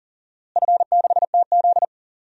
Morse code, heard as a keyed tone, about 40 wpm, 700 Hz. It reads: F6TZ
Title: DXCC-100 Call Signs - 4 Characters - Repeated 3x - 40wpm